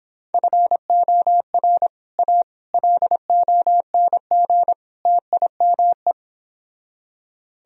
Morse code, heard as a keyed tone, about 26 wpm, 700 Hz. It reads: FOR A LONG TIME